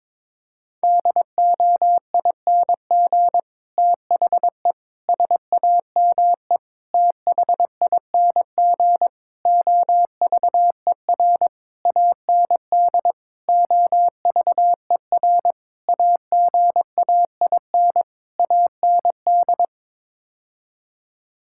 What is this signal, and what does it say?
Morse code, heard as a keyed tone, about 22 wpm, 700 Hz. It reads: DOING THE SAME THING OVER AND OVER AGAIN AND
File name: Sets of 9 Words - 22wpm